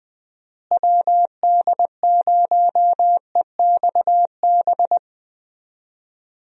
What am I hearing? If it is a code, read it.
Morse code, heard as a keyed tone, about 20 wpm, 700 Hz. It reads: WD0EXB
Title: USA Call Signs - 6 Characters - Repeated 2x - 20wpm